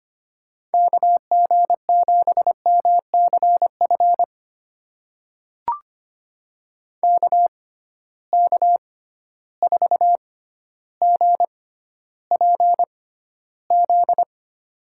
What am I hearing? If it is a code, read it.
Morse code, heard as a keyed tone, about 25 wpm, spaced as for 13 wpm, 700 Hz. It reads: KG7MCF KK4GPZ
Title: USA Call Signs - 6 Characters - 6x Character Spacing - 25wpm